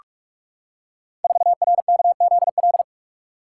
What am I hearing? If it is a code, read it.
Morse code, heard as a keyed tone, about 44 wpm, 700 Hz. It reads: 4RKZL
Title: Code Groups - 5 Letter:Numbers - 45wpm